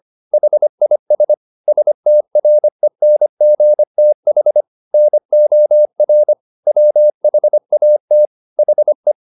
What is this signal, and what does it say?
Morse code, heard as a keyed tone, about 25 wpm, 600 Hz. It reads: HIS STRENGTH NOR WHAT HE
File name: the old man and the sea part 1 25wpm